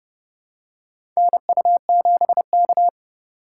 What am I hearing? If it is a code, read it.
Morse code, heard as a keyed tone, about 32 wpm, 700 Hz. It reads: NU7K
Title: USA Call Signs - 3x Character Spacing - 30wpm